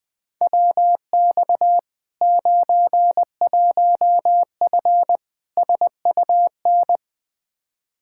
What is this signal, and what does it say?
Morse code, heard as a keyed tone, about 20 wpm, 700 Hz. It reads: WX 91F SUN
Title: Practice QSO - Exchange 2 - Info - 20wpm